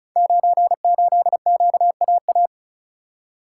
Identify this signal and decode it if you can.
Morse code, heard as a keyed tone, about 36 wpm, 700 Hz. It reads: 98QAA
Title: Code Groups - 5 Letter:Numbers - 35wpm